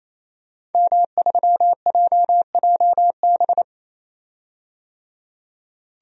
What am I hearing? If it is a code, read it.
Morse code, heard as a keyed tone, about 28 wpm, 700 Hz. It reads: M3JJB
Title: DXCC-100 Call Signs - 5 Characters - 28wpm